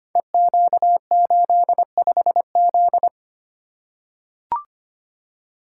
Morse code, heard as a keyed tone, about 25 wpm, 700 Hz. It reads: EQ85Z
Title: Code Groups - 5 Letter:Numbers - 25wpm